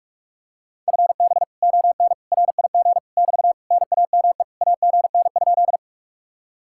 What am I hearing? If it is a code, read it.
Morse code, heard as a keyed tone, about 45 wpm, 700 Hz. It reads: FB ON RIG <BT> NAME AGN?